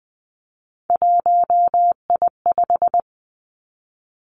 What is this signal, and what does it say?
Morse code, heard as a keyed tone, about 20 wpm, 700 Hz. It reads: 1I5
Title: Three Challenging Letter-Number Combinations 20wpm